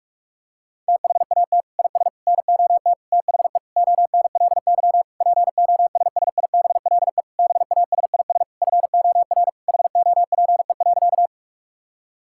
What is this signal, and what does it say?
Morse code, heard as a keyed tone, about 41 wpm, 700 Hz. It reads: THAT IS NOT THE ONLY POSSIBLE BASIS FOR HOPE.
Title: Sentences from Top 1000 Words - Review Part 5 - 45wpm